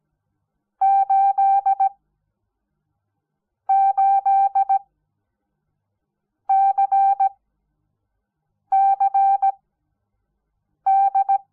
Morse code, heard as a keyed tone, about 17 wpm, spaced as for 5 wpm, 780 Hz. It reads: 88CCD
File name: Onlinelehrgang / morsen / DJ4UF Morsen06